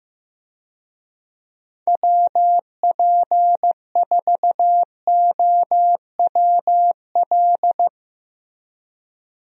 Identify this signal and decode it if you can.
Morse code, heard as a keyed tone, about 15 wpm, 700 Hz. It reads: WP4OWL
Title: USA Call Signs - 6 Characters - Repeated 2x - 15wpm